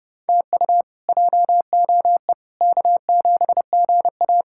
Morse code, heard as a keyed tone, about 32 wpm, 700 Hz. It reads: TU JOE K7GA
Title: K1USN SST QSO Practice - 30wpm